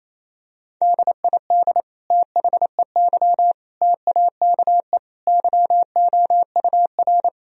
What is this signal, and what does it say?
Morse code, heard as a keyed tone, about 27 wpm, 700 Hz. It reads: DID THEY TAKE YOUR
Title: Sentences from Top 1000 Words - Review Part 3 - 28wpm